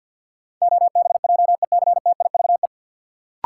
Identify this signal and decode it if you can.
Morse code, heard as a keyed tone, about 53 wpm, 700 Hz. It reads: OBJECTIVE
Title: Sentences from Top 1200 Words - New Word Drill - 50wpm